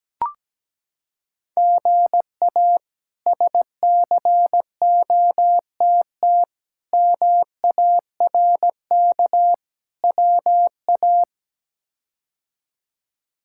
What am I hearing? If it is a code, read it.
Morse code, heard as a keyed tone, about 17 wpm, 700 Hz. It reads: GA SCOTT MARK WA
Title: K1USN SST QSO Practice - 17wpm